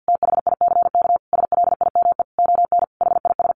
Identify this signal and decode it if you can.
Morse code, heard as a keyed tone, about 50 wpm, 700 Hz. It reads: THICK SLIME ON HIS
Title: A Sound Of Thunder 50wpm